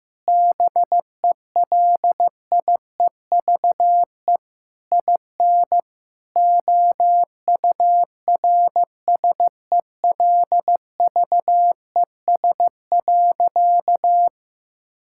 Morse code, heard as a keyed tone, about 15 wpm, 700 Hz. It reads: BELIEVE IN OURSELVES.
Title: Sentences from Top 1600 Words 15wpm